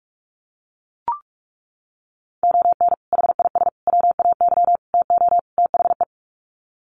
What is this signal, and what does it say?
Morse code, heard as a keyed tone, about 44 wpm, 690 Hz. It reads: ON HIS WAY TO THE
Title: Sets of 5 Words Encore2 45wpm